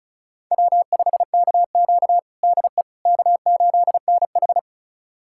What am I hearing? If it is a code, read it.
Morse code, heard as a keyed tone, about 34 wpm, 700 Hz. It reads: W5KQ DE K8NH
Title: USA Callsign DE Callsign - 4 Characters - 35wpm